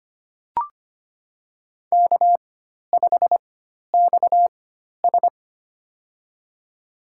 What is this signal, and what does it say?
Morse code, heard as a keyed tone, about 25 wpm, spaced as for 12 wpm, 700 Hz. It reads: K5XS
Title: USA Call Signs - 4x Character Spacing - 25wpm